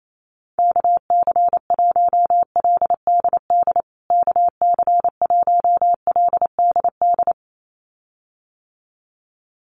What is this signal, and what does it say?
Morse code, heard as a keyed tone, about 28 wpm, 700 Hz. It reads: KC1LDD KC1LDD
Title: USA Call Signs Repeated 2x 28wpm